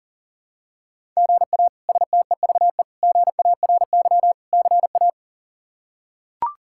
Morse code, heard as a keyed tone, about 37 wpm, 700 Hz. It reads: GA STEVE GARY CA
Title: K1USN SST QSO Practice - 40wpm